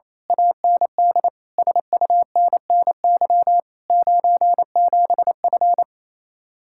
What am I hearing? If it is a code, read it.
Morse code, heard as a keyed tone, about 26 wpm, 700 Hz. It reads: AND SUNNY 97F
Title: Speed-Racing Common QSO Phrases - 28wpm